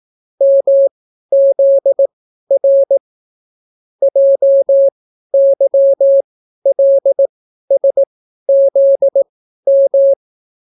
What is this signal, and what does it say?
Morse code, heard as a keyed tone, about 18 wpm, spaced as for 12 wpm, 550 Hz. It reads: MZR JYLSZM